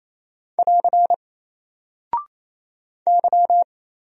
Morse code, heard as a keyed tone, about 27 wpm, 700 Hz. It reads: <AR> Y
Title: Single Character (Letter:Number:Punctuation:Prosign) 28wpm